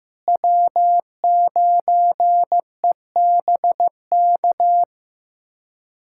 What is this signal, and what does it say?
Morse code, heard as a keyed tone, about 15 wpm, 700 Hz. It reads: W9EBK
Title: K1USN SST QSO Practice - 15wpm